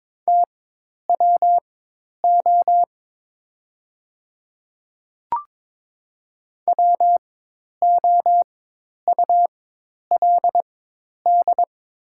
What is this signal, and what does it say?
Morse code, heard as a keyed tone, about 22 wpm, spaced as for 10 wpm, 700 Hz. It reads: TWO WOULD
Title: 4x Character Spacing - Top 100 Words - 22wpm